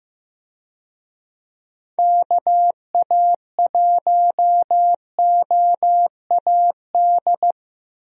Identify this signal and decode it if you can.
Morse code, heard as a keyed tone, about 15 wpm, 700 Hz. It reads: KA1OAD
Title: USA Call Signs - 6 Characters - Repeated 3x - 15wpm